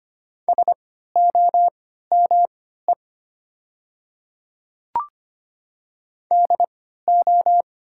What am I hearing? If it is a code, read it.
Morse code, heard as a keyed tone, about 25 wpm, spaced as for 14 wpm, 700 Hz. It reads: SOME DO
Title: 3x Character Spacing - Top 100 Words - 25wpm